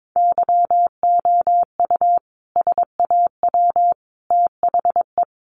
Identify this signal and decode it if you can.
Morse code, heard as a keyed tone, about 22 wpm, 700 Hz. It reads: YOU SAW THE
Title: Sentences from Top 400 Words 22wpm